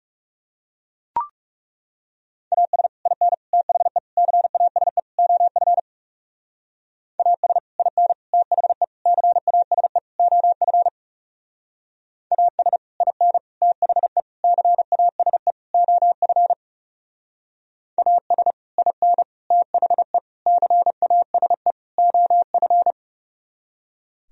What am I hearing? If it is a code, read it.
Morse code, heard as a keyed tone, about 39 wpm, 700 Hz. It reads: AS IN THE CASE OF AS IN THE CASE OF AS IN THE CASE OF AS IN THE CASE OF